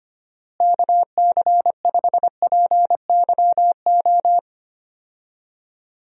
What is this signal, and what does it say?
Morse code, heard as a keyed tone, about 25 wpm, 700 Hz. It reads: KC5PYO
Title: USA Call Signs - 6 Characters - Repeated 3x - 25wpm